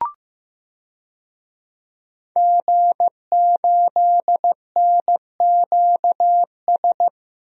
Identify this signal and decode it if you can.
Morse code, heard as a keyed tone, about 15 wpm, 700 Hz. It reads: G8NQS
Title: DXCC-100 Call Signs - 5 Characters - 15wpm